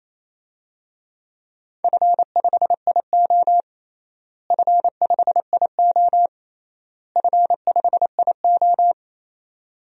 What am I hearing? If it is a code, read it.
Morse code, heard as a keyed tone, about 28 wpm, 700 Hz. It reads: F5IO F5IO F5IO
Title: DXCC-100 Call Signs - 4 Characters - Repeated 3x - 28wpm